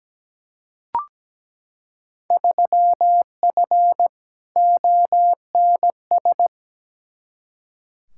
Morse code, heard as a keyed tone, about 17 wpm, 700 Hz. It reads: 3F ONS
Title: Field Day Exchanges - 17wpm